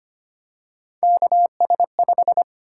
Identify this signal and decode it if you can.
Morse code, heard as a keyed tone, about 25 wpm, 700 Hz. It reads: KS5
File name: Three Letter-Number Combinations - 25wpm